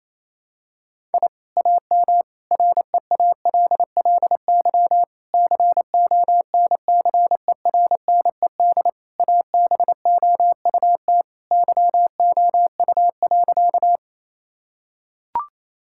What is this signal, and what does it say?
Morse code, heard as a keyed tone, about 28 wpm, 700 Hz. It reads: I AM REALLY CONCERNED ABOUT YOU.
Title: Sentences from Top 1000 Words - Review Part 6 - 28wpm